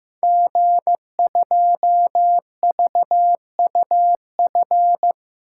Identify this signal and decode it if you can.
Morse code, heard as a keyed tone, about 15 wpm, 700 Hz. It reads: G2VUF
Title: Code Groups - 5 Letter:Numbers - 15wpm